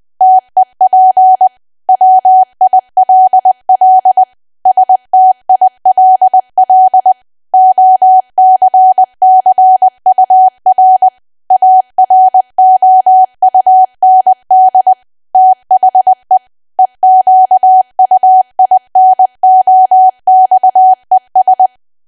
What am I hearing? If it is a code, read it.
Morse code, heard as a keyed tone, about 20 wpm, 750 Hz. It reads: TEP WILL STILL OCCUR AROUND THE EQUINOXES
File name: cwa / W1AW-Level3 / 20WPM